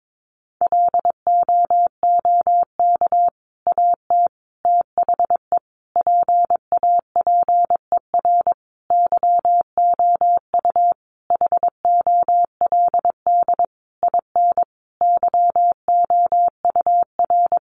Morse code, heard as a keyed tone, about 22 wpm, 700 Hz. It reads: LOOK AT THE PAPER YOU HOLD IN YOUR
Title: Sentences from Top 400 Words 22wpm